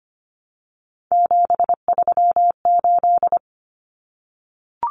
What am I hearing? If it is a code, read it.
Morse code, heard as a keyed tone, about 25 wpm, 700 Hz. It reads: 738
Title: Numbers - 3-Digit 25wpm